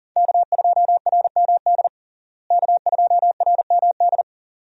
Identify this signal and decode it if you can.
Morse code, heard as a keyed tone, about 40 wpm, 700 Hz. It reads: K2RMD K2RMD
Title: USA Call Signs - 5 Characters - Repeated 2x - 40wpm